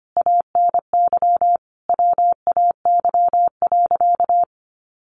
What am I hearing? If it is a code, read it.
Morse code, heard as a keyed tone, about 25 wpm, 700 Hz. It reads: ANY WAY.
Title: Sentences from Top 600 Words 25wpm